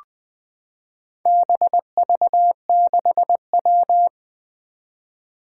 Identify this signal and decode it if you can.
Morse code, heard as a keyed tone, about 20 wpm, 700 Hz. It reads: BV6W
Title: Four Challenging Letter-Number Combinations 20wpm